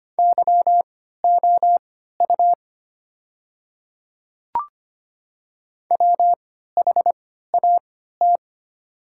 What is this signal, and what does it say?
Morse code, heard as a keyed tone, about 25 wpm, spaced as for 14 wpm, 700 Hz. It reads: YOU WHAT
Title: 3x Character Spacing - Top 100 Words - 25wpm